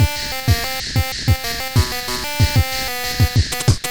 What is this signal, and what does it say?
There is no code.